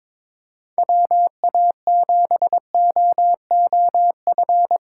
Morse code, heard as a keyed tone, about 25 wpm, 700 Hz. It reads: WA7OOF
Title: Speed Racing - USA Call Signs 22wpm